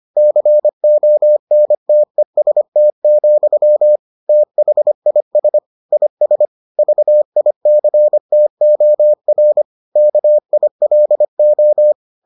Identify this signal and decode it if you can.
Morse code, heard as a keyed tone, about 25 wpm, 600 Hz. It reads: CONTEST, THIS IS VICTOR KILO